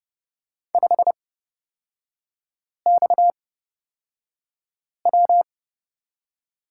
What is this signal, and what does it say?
Morse code, heard as a keyed tone, about 30 wpm, spaced as for 5 wpm, 690 Hz. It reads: 5XW
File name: Lesson 37e - Letter x - Three Characters 30wpm